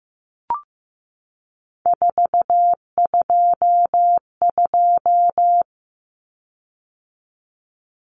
Morse code, heard as a keyed tone, about 15 wpm, 700 Hz. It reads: 422
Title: Common Cut Numbers - 3-Digit 15wpm